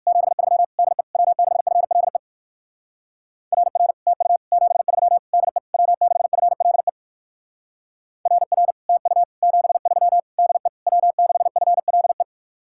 Morse code, heard as a keyed tone, about 59 wpm, 700 Hz. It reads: 73 DE W6FLE RR TU 73 DE W6FLE RR TU 73 DE W6FLE